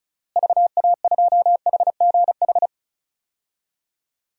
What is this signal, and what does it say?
Morse code, heard as a keyed tone, about 36 wpm, 700 Hz. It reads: VA2HGH
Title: DXCC-100 Call Signs - 6 Characters - Repeated 2x - 35wpm